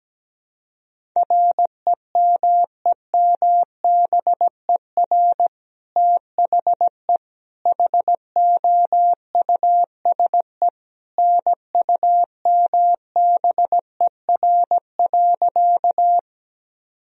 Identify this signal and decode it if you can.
Morse code, heard as a keyed tone, about 17 wpm, 700 Hz. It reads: REMEMBER THE HOUSE NUMBER.
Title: Sentences from Top 1000 Words - Review Part 1 - 17wpm